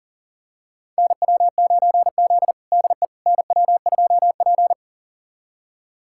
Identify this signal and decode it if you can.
Morse code, heard as a keyed tone, about 40 wpm, 700 Hz. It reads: NW9Z DE NW2P
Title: USA Callsign DE Callsign - 4 Characters - 40wpm